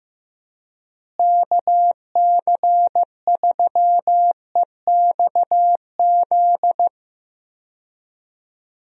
Morse code, heard as a keyed tone, about 15 wpm, 700 Hz. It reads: KC3EXZ
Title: USA Call Signs - 6 Characters - Repeated 3x - 15wpm